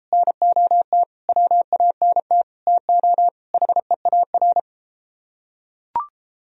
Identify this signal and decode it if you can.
Morse code, heard as a keyed tone, about 31 wpm, 700 Hz. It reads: NOT WANT TO HEAR